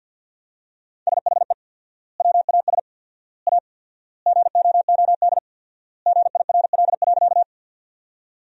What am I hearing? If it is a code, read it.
Morse code, heard as a keyed tone, about 50 wpm, 700 Hz. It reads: SHE WAS A GOOD GIRL.
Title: 4x Word-Spacing - Sentences from Top 400 Words - 50wpm